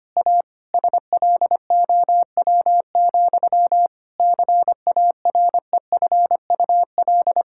Morse code, heard as a keyed tone, about 25 wpm, 700 Hz. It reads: A SLOW, CAREFUL
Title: Sentences from Top 1600 Words 25wpm